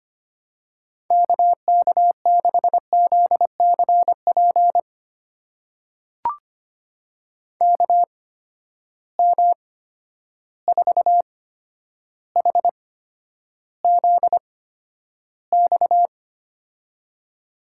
Morse code, heard as a keyed tone, about 25 wpm, spaced as for 11 wpm, 700 Hz. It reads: KK6ZCP KM4HZX
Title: USA Call Signs - 6 Characters - 8x Character Spacing - 25wpm